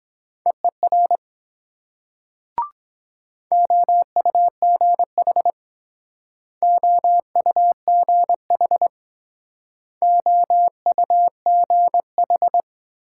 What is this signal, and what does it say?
Morse code, heard as a keyed tone, about 23 wpm, 700 Hz. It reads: EER OUGH OUGH OUGH